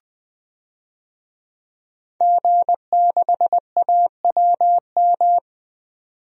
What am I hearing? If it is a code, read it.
Morse code, heard as a keyed tone, about 20 wpm, 700 Hz. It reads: G6AWM
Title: DXCC-100 Call Signs - 5 Characters - 20wpm